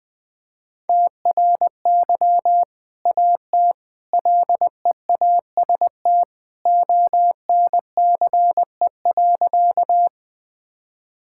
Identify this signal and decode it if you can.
Morse code, heard as a keyed tone, about 20 wpm, 700 Hz. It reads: TRY AT LEAST ONCE.
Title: Sentences with 2-Word Phrases - 20wpm